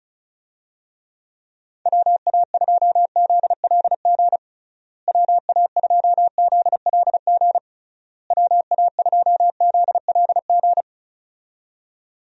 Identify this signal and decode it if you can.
Morse code, heard as a keyed tone, about 34 wpm, 700 Hz. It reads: WA2ZLG WA2ZLG WA2ZLG
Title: USA Call Signs - 6 Characters - Repeated 3x - 35wpm